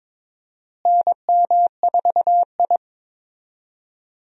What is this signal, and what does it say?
Morse code, heard as a keyed tone, about 22 wpm, 700 Hz. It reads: NM4I
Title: USA Call Signs - 4 Characters - Repeated 3x - 22wpm